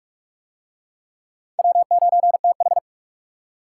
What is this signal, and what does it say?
Morse code, heard as a keyed tone, about 39 wpm, 700 Hz. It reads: W9TH